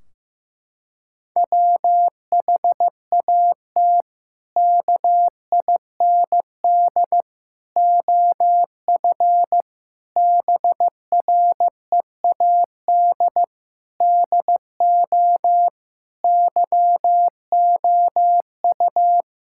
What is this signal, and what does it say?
Morse code, heard as a keyed tone, about 15 wpm, 700 Hz. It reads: WHAT KIND OF BREAD DO YOU
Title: Sentences from Top 1600 Words 15wpm